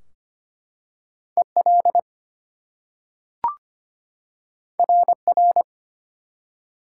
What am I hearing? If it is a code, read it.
Morse code, heard as a keyed tone, about 25 wpm, 700 Hz. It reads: EL RR
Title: Top 180 QSO Elements 25wpm